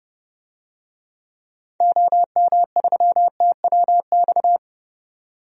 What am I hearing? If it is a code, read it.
Morse code, heard as a keyed tone, about 31 wpm, 700 Hz. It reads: OM3TWX
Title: DXCC-100 Call Signs - 6 Characters - 30wpm